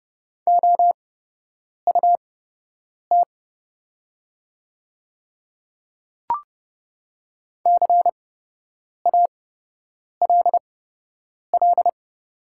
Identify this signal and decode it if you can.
Morse code, heard as a keyed tone, about 30 wpm, spaced as for 8 wpm, 700 Hz. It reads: OUT CALL
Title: 8x Character Spacing - Top 100 Words - 30wpm